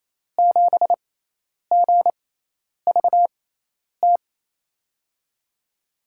Morse code, heard as a keyed tone, about 28 wpm, spaced as for 10 wpm, 700 Hz. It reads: 7GVT
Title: USA Call Signs - 6 Characters - 6x Character Spacing - 28wpm